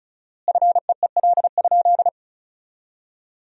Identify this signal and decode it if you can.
Morse code, heard as a keyed tone, about 34 wpm, 700 Hz. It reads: FEEL?
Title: 4x Word-Spacing - Sentences from Top 400 Words - 35wpm